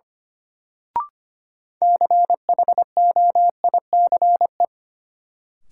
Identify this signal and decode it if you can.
Morse code, heard as a keyed tone, about 25 wpm, 700 Hz. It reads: CHOICE
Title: Sets of 5 Words - New Word Drill - 25wpm